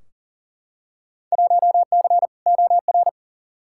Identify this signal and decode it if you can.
Morse code, heard as a keyed tone, about 40 wpm, 700 Hz. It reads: 1C OR
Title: Field Day Exchanges - 40wpm